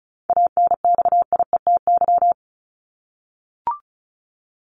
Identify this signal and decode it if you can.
Morse code, heard as a keyed tone, about 34 wpm, 700 Hz. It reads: ANXIETY
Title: Top 1501-1750 Words 35wpm